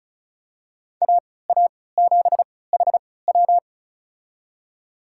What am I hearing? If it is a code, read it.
Morse code, heard as a keyed tone, about 34 wpm, spaced as for 20 wpm, 700 Hz. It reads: AA7HW
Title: USA Call Signs - 5 Characters - 3x Character Spacing - 35wpm